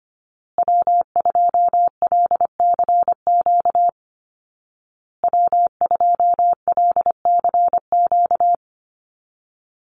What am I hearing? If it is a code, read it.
Morse code, heard as a keyed tone, about 25 wpm, 700 Hz. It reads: W2LCQ W2LCQ
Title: Call Signs Repeated 2x - 25wpm